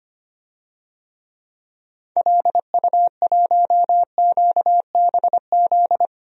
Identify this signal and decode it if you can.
Morse code, heard as a keyed tone, about 25 wpm, 700 Hz. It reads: LU1QBZ
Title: DXCC-100 Call Signs - 6 Characters - 25wpm